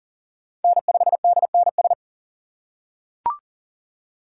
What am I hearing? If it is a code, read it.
Morse code, heard as a keyed tone, about 40 wpm, 700 Hz. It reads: N5DNS
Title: USA Call Signs - 4x Character Spacing - 40wpm